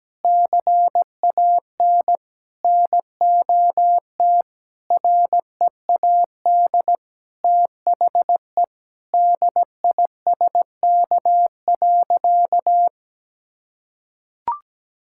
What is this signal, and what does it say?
Morse code, heard as a keyed tone, about 17 wpm, 700 Hz. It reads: CAN NOT READ THE DISK.
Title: Sentences from Top 1000 Words - Review Part 6 - 17wpm